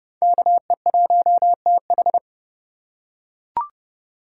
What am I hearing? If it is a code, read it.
Morse code, heard as a keyed tone, about 29 wpm, 700 Hz. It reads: KE1TH